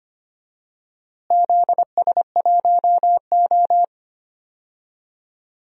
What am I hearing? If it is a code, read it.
Morse code, heard as a keyed tone, about 25 wpm, 700 Hz. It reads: ZS1O
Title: DXCC-100 Call Signs - 4 Characters - Repeated 3x - 25wpm